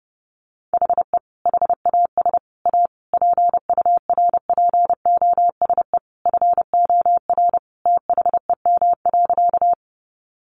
Morse code, heard as a keyed tone, about 28 wpm, 700 Hz. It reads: HE HAS A PURPOSE FOR THEM.
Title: Sentences from Top 800 Words 30wpm